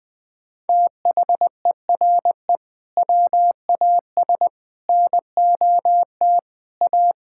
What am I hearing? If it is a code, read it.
Morse code, heard as a keyed tone, about 21 wpm, 700 Hz. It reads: THERE WAS NOT A
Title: Sentences from Top 1000 Words - Review Part 6 - 20wpm